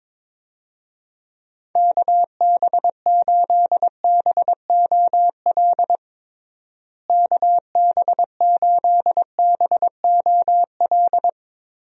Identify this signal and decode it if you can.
Morse code, heard as a keyed tone, about 22 wpm, 700 Hz. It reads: KB8BOL KB8BOL
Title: USA Call Signs - 6 Characters - Repeated 2x - 22wpm